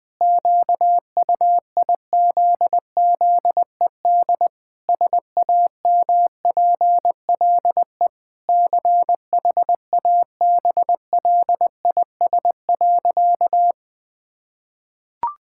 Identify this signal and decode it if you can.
Morse code, heard as a keyed tone, about 20 wpm, 700 Hz. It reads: QUIZZED SAMPLE CHABLIS.
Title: Pangram Challenge v2 - 20wpm